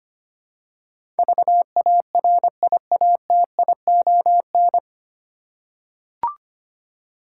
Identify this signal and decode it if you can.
Morse code, heard as a keyed tone, about 25 wpm, 700 Hz. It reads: VARIATION
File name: Nine Letter Words - 25wpm